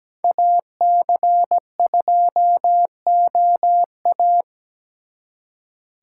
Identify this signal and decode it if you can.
Morse code, heard as a keyed tone, about 20 wpm, 700 Hz. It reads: AC2OA
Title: Speed Racing - USA Call Signs 17wpm